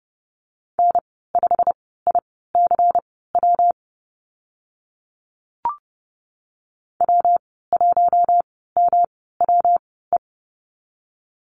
Morse code, heard as a keyed tone, about 30 wpm, spaced as for 17 wpm, 700 Hz. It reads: N5ICW W1MWE
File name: Lesson 19g - Letter w - Call Signs 30wpm